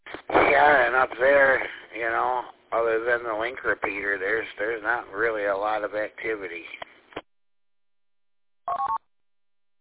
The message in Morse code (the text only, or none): none